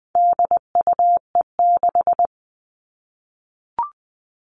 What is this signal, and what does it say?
Morse code, heard as a keyed tone, about 20 wpm, 700 Hz. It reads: DUE6
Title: All Cut Numbers - 4-Digit 20wpm